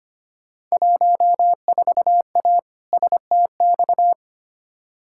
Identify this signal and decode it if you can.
Morse code, heard as a keyed tone, about 25 wpm, 700 Hz. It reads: 14A STX
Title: Field Day Exchanges Encore - 25wpm